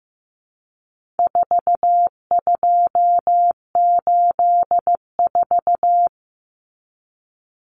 Morse code, heard as a keyed tone, about 15 wpm, 700 Hz. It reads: 4284
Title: Common Cut Numbers - 4-Digit 15wpm